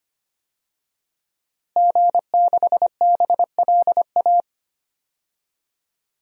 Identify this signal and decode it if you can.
Morse code, heard as a keyed tone, about 25 wpm, 700 Hz. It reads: G6BLA
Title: DXCC-100 Call Signs - 5 Characters - Repeated 2x - 25wpm